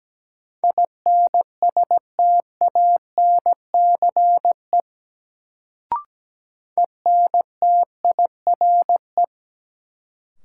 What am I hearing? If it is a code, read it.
Morse code, heard as a keyed tone, about 17 wpm, 700 Hz. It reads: INSTANCE ENTIRE
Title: Sentences from Top 800 Words - New Word Drill - 17wpm